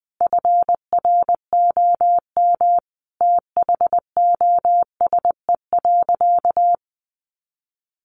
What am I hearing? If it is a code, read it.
Morse code, heard as a keyed tone, about 20 wpm, 700 Hz. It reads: FROM THOSE.
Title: A Sound Of Thunder 20wpm